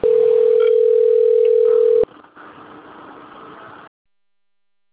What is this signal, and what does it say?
Ringback tone (North America), 440 + 480 Hz.